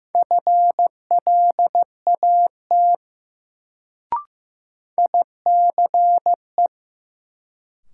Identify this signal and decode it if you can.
Morse code, heard as a keyed tone, about 15 wpm, 700 Hz. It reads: FLAT ICE
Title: Sentences from Top 1000 Words - New Word Drill - 15wpm